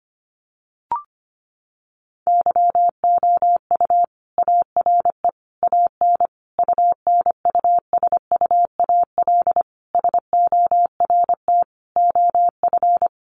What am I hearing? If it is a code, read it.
Morse code, heard as a keyed tone, about 25 wpm, 700 Hz. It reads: YOU ARE AN UNUSUAL SORT OF
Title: Sentences from Top 1400 Words 25wpm